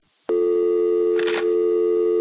Dial tone (North America), 350 + 440 Hz.
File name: phone numbers / calls / michelle 01 var2